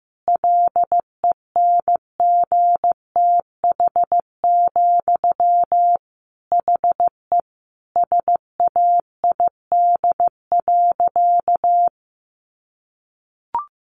Morse code, heard as a keyed tone, about 15 wpm, 700 Hz. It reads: LENGTH, HE SAID.
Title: Sentences from Top 1000 Words 15wpm